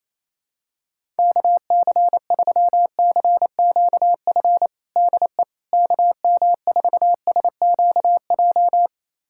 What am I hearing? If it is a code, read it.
Morse code, heard as a keyed tone, about 28 wpm, 700 Hz. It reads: KC3CQF DE KM4SQJ
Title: USA Callsign DE Callsign - 6 Characters - 28wpm